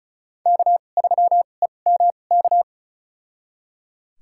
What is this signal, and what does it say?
Morse code, heard as a keyed tone, about 36 wpm, spaced as for 25 wpm, 700 Hz. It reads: K3EMK
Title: USA Call Signs - 5 Characters - 2x Character Spacing - 35wpm